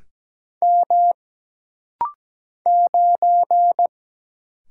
Morse code, heard as a keyed tone, about 17 wpm, 700 Hz. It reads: M 9
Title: Single Letter-Number - Rapid-Fire 17wpm